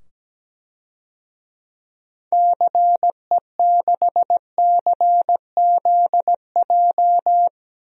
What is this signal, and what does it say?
Morse code, heard as a keyed tone, about 17 wpm, 700 Hz. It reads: CE6CZJ
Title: DXCC-100 Call Signs - 6 Characters - 17wpm